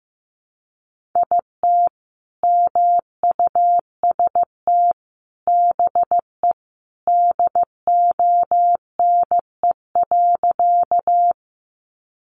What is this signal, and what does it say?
Morse code, heard as a keyed tone, about 15 wpm, 700 Hz. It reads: IT MUST BE DONE.
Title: Sentences from Top 500 Words 15wpm